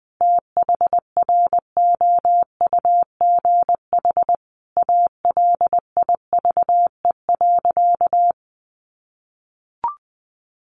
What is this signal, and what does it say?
Morse code, heard as a keyed tone, about 19 wpm, 700 Hz. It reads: THROUGH ALIVE.
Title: A Sound Of Thunder 20wpm